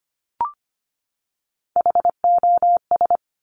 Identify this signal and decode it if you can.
Morse code, heard as a keyed tone, about 25 wpm, 700 Hz. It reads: HOS
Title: Top Three Letter Combinations in Words 25wpm